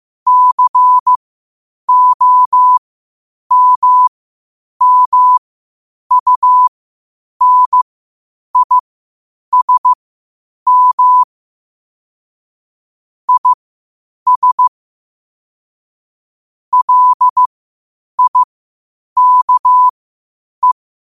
Morse code, heard as a keyed tone, about 15 wpm, spaced as for 8 wpm, 1000 Hz. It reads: COMMUNISM IS LIKE
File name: qotdfiles / 8wpm